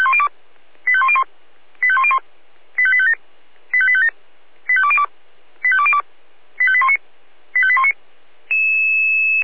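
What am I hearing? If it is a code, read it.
Five-tone selective call (ZVEI), heard as quick runs of five tones, tones of about 70 ms. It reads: 85101 85101 85857 85857 85202 85202 85819 85819
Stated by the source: ZVEI five-tone selective call 85810 received on an SDR.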